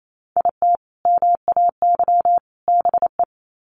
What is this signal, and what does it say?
Morse code, heard as a keyed tone, about 28 wpm, 700 Hz. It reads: IT MAY BE
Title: Sets of 3 Words 28wpm